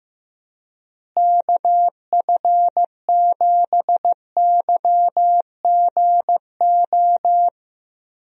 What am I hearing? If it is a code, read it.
Morse code, heard as a keyed tone, about 15 wpm, 700 Hz. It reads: KF7YGO
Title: USA Call Signs - 6 Characters - Repeated 2x - 15wpm